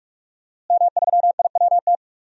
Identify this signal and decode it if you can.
Morse code, heard as a keyed tone, about 46 wpm, 700 Hz. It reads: M3IWT
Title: DXCC-100 Call Signs - 5 Characters - Repeated 3x - 45wpm